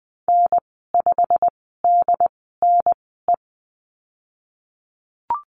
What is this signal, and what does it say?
Morse code, heard as a keyed tone, about 20 wpm, spaced as for 15 wpm, 700 Hz. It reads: N5DNE
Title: 161 Lesson 26g - Letter g - Call Signs 20wpm